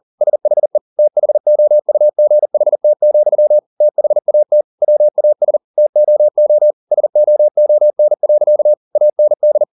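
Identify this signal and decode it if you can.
Morse code, heard as a keyed tone, about 40 wpm, 600 Hz. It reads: SHE THOUGHT, THAT WAS TOO SOON. AND
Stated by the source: Morse code at 40 wpm, practice text, CW med Gnister.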